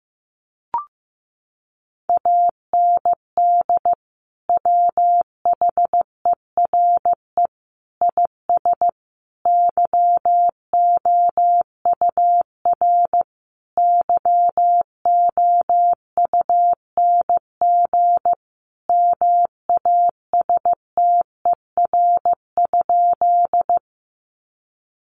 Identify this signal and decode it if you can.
Morse code, heard as a keyed tone, about 14 wpm, 700 Hz. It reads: AND WHERE IS YOUR YOUNG MASTER?
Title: Sentences from Top 1300 Words 15wpm